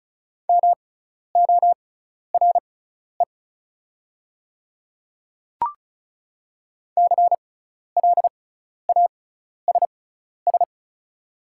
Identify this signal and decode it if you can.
Morse code, heard as a keyed tone, about 36 wpm, spaced as for 12 wpm, 690 Hz. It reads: MORE CLASS
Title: 6x Character Spacing - Top 500 Words - 35wpm